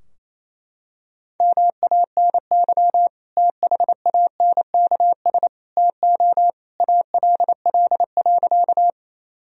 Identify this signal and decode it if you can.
Morse code, heard as a keyed tone, about 28 wpm, 700 Hz. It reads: MANY THANKS TO ALL.
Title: Sentences from Top 1000 Words - Review Part 3 - 28wpm